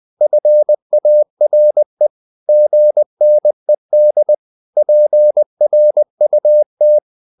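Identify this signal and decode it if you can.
Morse code, heard as a keyed tone, about 20 wpm, 600 Hz. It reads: FARE GNED PRUT
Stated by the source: Morse code practice of short Danish words at 20 wpm, from CW med Gnister.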